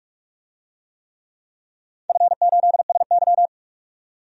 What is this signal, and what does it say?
Morse code, heard as a keyed tone, about 47 wpm, 700 Hz. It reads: F8SY
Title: DXCC-100 Call Signs - 4 Characters - 45wpm